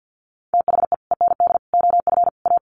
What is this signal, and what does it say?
Morse code, heard as a keyed tone, about 50 wpm, 700 Hz. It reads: THE END OF A
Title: Sets of 5 Words Encore2 50wpm